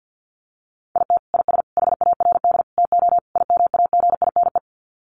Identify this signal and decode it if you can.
Morse code, heard as a keyed tone, about 50 wpm, 700 Hz. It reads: IT IS HARD TO IMAGINE
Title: Sets of 5 Words Encore2 50wpm